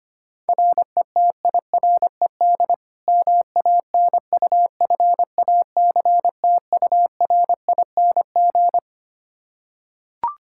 Morse code, heard as a keyed tone, about 25 wpm, 700 Hz. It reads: RETIRED MANUFACTURING
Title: Practice QSO - Exchange 3 - More Info - 25wpm